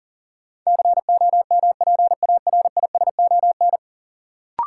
Morse code, heard as a keyed tone, about 40 wpm, 700 Hz. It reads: COMPARISON
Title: Sentences from Top 1600 Words - New Word Drill Alt - 40wpm